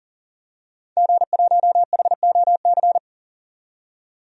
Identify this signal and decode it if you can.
Morse code, heard as a keyed tone, about 40 wpm, 700 Hz. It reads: G1HOC
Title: DXCC-100 Call Signs - 5 Characters - 40wpm